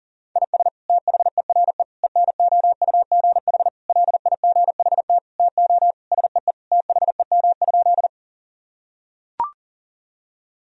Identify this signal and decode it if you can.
Morse code, heard as a keyed tone, about 40 wpm, 700 Hz. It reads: IS THERE ENOUGH LIGHT TO SEE THEM?
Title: Sentences from Top 1000 Words - Review Part 1 - 40wpm